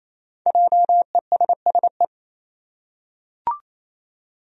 Morse code, heard as a keyed tone, about 28 wpm, 700 Hz. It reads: JESSE
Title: USA Common Names - 28wpm